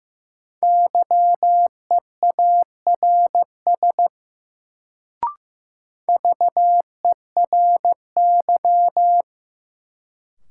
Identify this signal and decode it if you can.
Morse code, heard as a keyed tone, about 15 wpm, 700 Hz. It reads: YEARS VERY
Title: Sentences from Top 200 Words - New Word Drill - 15wpm